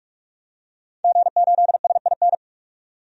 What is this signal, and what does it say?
Morse code, heard as a keyed tone, about 46 wpm, 700 Hz. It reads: G8SIN
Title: DXCC-100 Call Signs - 5 Characters - 45wpm